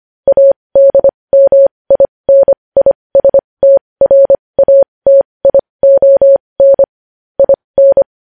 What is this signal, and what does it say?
Morse code, heard as a keyed tone, about 25 wpm, spaced as for 20 wpm, 550 Hz. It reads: ADMINISTRATION IN
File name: podcast / News headlines-1x-25x20